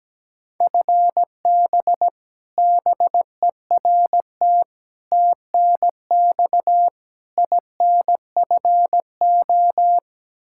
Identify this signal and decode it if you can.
Morse code, heard as a keyed tone, about 17 wpm, 700 Hz. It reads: FB BERT TNX INFO
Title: Practice QSO - Exchange 3 - More Info - 17wpm